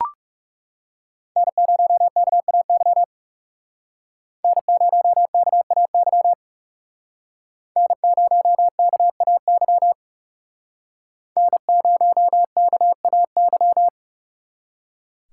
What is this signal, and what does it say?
Morse code, heard as a keyed tone, about 36 wpm, 700 Hz. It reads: N0KAY N0KAY N0KAY N0KAY
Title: Speed Racing - USA 1x3 Call Signs 30wpm